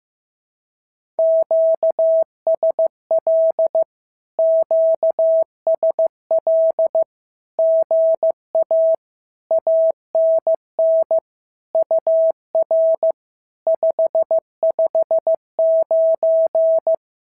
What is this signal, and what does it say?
Morse code, heard as a keyed tone, about 15 wpm, 650 Hz. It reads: QSL QSL GA ANN UR 559
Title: SOTA QSO Practice 15wpm